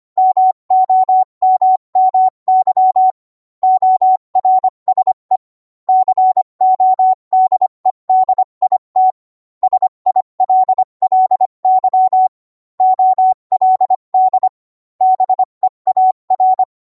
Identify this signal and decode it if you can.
Morse code, heard as a keyed tone, about 25 wpm, 750 Hz. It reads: MOMMY ORSE CODE DIT SILLY OLD BEAR